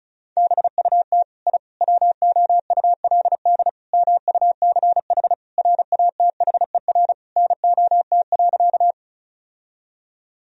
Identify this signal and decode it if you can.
Morse code, heard as a keyed tone, about 36 wpm, 700 Hz. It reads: BUT I WOULD MUCH RATHER NOT.
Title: Sentences from Top 1000 Words - Review Part 4 - 35wpm